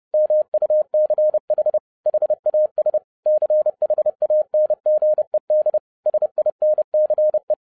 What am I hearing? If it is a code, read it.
Morse code, heard as a keyed tone, about 30 wpm, 610 Hz. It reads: MUCH HAS CHANGED SINCE